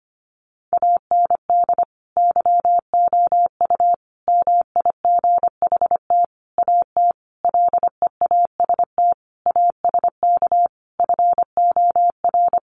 Morse code, heard as a keyed tone, about 25 wpm, 700 Hz. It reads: AND YOU MIGHT AT LEAST ASK FOR
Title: Sentences from Top 1500 Words 25wpm